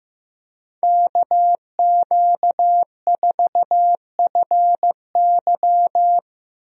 Morse code, heard as a keyed tone, about 15 wpm, 700 Hz. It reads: KQ4FY
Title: USA Call Signs - 5 Characters - Repeated 2x - 15wpm